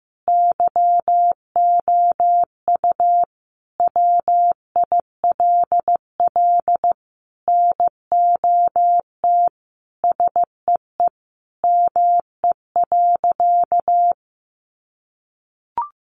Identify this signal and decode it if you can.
Morse code, heard as a keyed tone, about 15 wpm, 700 Hz. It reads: YOU WILL NOT SEE ME.
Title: Sentences from Top 600 Words 15wpm